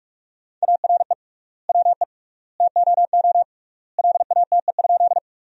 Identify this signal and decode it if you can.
Morse code, heard as a keyed tone, about 43 wpm, 700 Hz. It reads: ARE WE TOO LATE?
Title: 3x Word-Spacing - Sentences from Top 300 Words - 45wpm